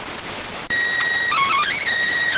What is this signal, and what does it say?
SSTV (slow-scan television), PD 290.